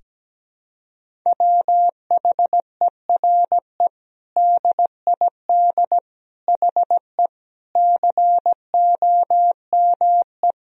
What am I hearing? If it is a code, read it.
Morse code, heard as a keyed tone, about 17 wpm, 700 Hz. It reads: WHERE DID HE COME
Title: Sentences from Top 200 Words and Abbreviations 17wpm